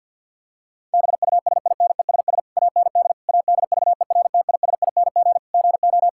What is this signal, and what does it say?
Morse code, heard as a keyed tone, about 51 wpm, 700 Hz. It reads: BUSINESS AND ADVERTISING GO
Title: Sentences from Top 1000 Words - Review Part 6 - 50wpm